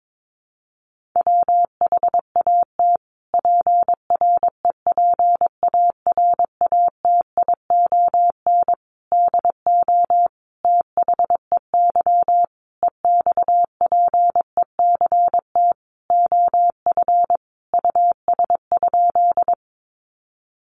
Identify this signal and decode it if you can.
Morse code, heard as a keyed tone, about 22 wpm, 700 Hz. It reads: WHAT PREPARATION DO THEY EXPECT OF US?
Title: Sentences from Top 1400 Words 22wpm